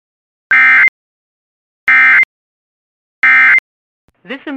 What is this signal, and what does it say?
Emergency Alert System (SAME) tones, heard as harsh buzzing bursts. It is heard from 0.5 to 3.6 s.